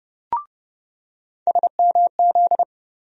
Morse code, heard as a keyed tone, about 30 wpm, 700 Hz. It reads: SMZ
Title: USA Call Sign Postfixes - 3 Letters 30wpm